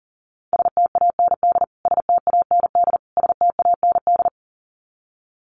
Morse code, heard as a keyed tone, about 40 wpm, 700 Hz. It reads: STAND STAND STAND
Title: Top 500 Words Repeated 40wpm